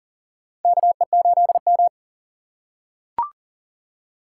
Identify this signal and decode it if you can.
Morse code, heard as a keyed tone, about 40 wpm, 700 Hz. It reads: KE8M
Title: K1USN SST QSO Practice - 40wpm